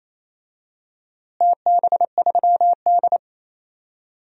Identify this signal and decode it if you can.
Morse code, heard as a keyed tone, about 28 wpm, 700 Hz. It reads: TB3D
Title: DXCC-100 Call Signs - 4 Characters - Repeated 2x - 28wpm